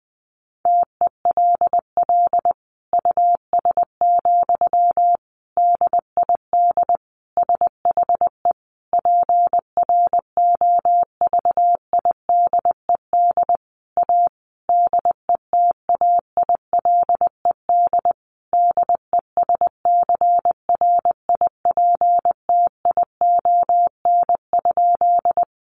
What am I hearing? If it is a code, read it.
Morse code, heard as a keyed tone, about 20 wpm, 700 Hz. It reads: TELL US, DID SHE PROVIDED A DETAILED DESCRIPTION?
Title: Sentences from Top 1500 Words 20wpm